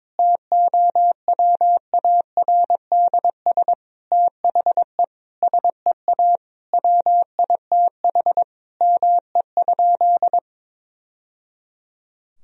Morse code, heard as a keyed tone, about 22 wpm, 700 Hz. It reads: TOWARDS THE SEA WITH ME?
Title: Sentences from Top 1000 Words - Review Part 2 - 22wpm